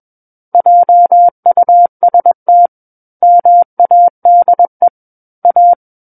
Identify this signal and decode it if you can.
Morse code, heard as a keyed tone, about 21 wpm, 700 Hz. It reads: JUST MADE A
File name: Book One / 12-Chapter 12 - 21 WPM